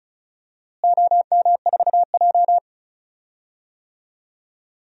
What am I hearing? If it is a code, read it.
Morse code, heard as a keyed tone, about 36 wpm, 690 Hz. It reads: OM4J
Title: DXCC-100 Call Signs - 4 Characters - Repeated 3x - 35wpm